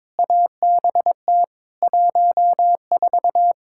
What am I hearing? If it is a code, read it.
Morse code, heard as a keyed tone, about 21 wpm, 700 Hz. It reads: ABT 14
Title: K1USN SST QSO Practice 22wpm